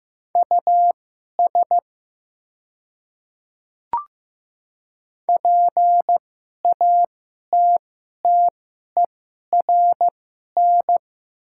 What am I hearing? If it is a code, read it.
Morse code, heard as a keyed tone, about 15 wpm, spaced as for 11 wpm, 700 Hz. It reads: US PATTERN
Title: 2x Character Spacing - Top 500 Words - 15wpm